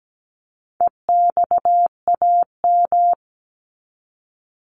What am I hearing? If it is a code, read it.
Morse code, heard as a keyed tone, about 17 wpm, 700 Hz. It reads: EXAM
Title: Top 501-750 Words 17wpm